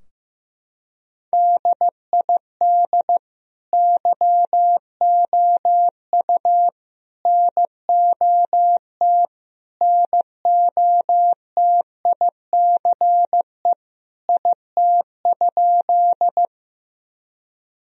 Morse code, heard as a keyed tone, about 16 wpm, 700 Hz. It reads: DID YOU NOT NOTICE IT?
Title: Sentences from Top 1000 Words - Review Part 2 - 15wpm